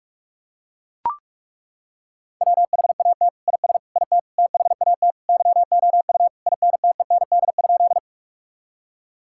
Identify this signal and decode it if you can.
Morse code, heard as a keyed tone, about 41 wpm, 700 Hz. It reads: WHAT IS IT THAT YOU INTEND?
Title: Sentences from Top 1600 Words 45wpm